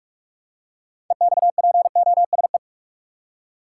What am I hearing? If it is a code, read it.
Morse code, heard as a keyed tone, about 42 wpm, 700 Hz. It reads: EXPOSE
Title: Six Letter Words - 45wpm